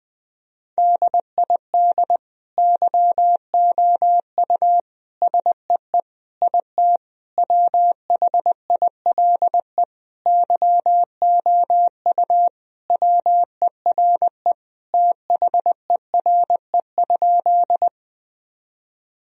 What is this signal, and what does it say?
Morse code, heard as a keyed tone, about 20 wpm, 700 Hz. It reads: DID YOU SEE IT WHILE YOU WERE THERE?
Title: Sentences from Top 1000 Words - Review Part 1 - 20wpm